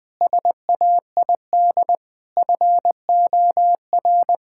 Morse code, heard as a keyed tone, about 20 wpm, 700 Hz. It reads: SAID FOR
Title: Sets of 6 Words - 20wpm